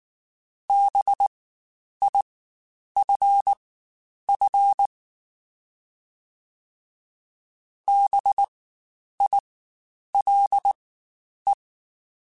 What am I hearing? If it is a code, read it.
Morse code, heard as a keyed tone, about 19 wpm, spaced as for 9 wpm, 800 Hz. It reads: BIFF BILE
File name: Spaced Alphas / Spaced Alphas mp3s / Bs sp @18